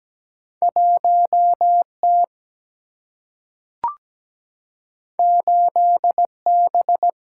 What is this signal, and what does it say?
Morse code, heard as a keyed tone, about 17 wpm, 700 Hz. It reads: 1 T 8 B
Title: Single-Letter Single-Number 17wpm